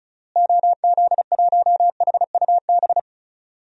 Morse code, heard as a keyed tone, about 36 wpm, 700 Hz. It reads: OZ1HUB
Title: DXCC-100 Call Signs - 6 Characters - Repeated 2x - 35wpm